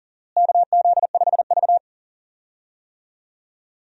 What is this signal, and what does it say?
Morse code, heard as a keyed tone, about 40 wpm, 700 Hz. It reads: KZ5V
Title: USA Call Signs - 4 Characters - Repeated 3x - 40wpm